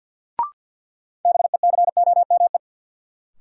Morse code, heard as a keyed tone, about 52 wpm, 700 Hz. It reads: BECOME
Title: Sentences from Top 600 Words - New Word Drill - 50wpm